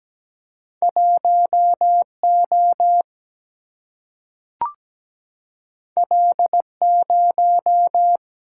Morse code, heard as a keyed tone, about 17 wpm, 700 Hz. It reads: 1 O L 0
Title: Single-Letter Single-Number 17wpm